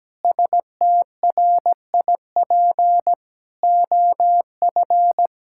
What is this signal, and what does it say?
Morse code, heard as a keyed tone, about 17 wpm, 700 Hz. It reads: STRIP OF
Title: Sentences from Top 1600 Words 17wpm